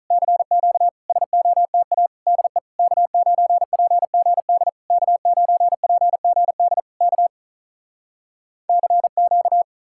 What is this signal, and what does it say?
Morse code, heard as a keyed tone, about 48 wpm, 700 Hz. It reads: CQ SOTA DE K9PGD K9PGD K CQ